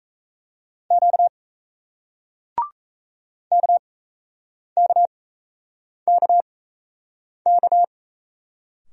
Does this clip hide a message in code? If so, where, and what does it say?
Yes: Morse code from 0.9 to 7.8 s, heard as a keyed tone, about 36 wpm, spaced as for 8 wpm, 700 Hz. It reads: Q KKKK